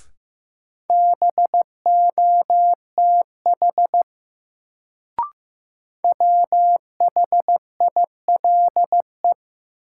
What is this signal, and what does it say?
Morse code, heard as a keyed tone, about 15 wpm, 700 Hz. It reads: BOTH WHILE
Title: Sentences from Top 300 Words - New Word Drill - 15wpm